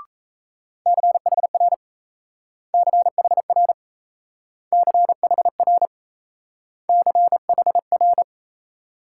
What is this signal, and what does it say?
Morse code, heard as a keyed tone, about 34 wpm, 700 Hz. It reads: CHR CHR CHR CHR
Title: Speed-Racing Multi-letter Phonemes - 28wpm